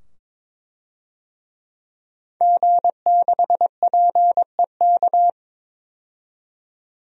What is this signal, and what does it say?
Morse code, heard as a keyed tone, about 22 wpm, 700 Hz. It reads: G6PEK
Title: DXCC-100 Call Signs - 5 Characters - 22wpm